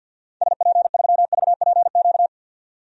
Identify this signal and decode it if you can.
Morse code, heard as a keyed tone, about 52 wpm, 700 Hz. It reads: SP3VPQ
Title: DXCC-100 Call Signs - 6 Characters - 50wpm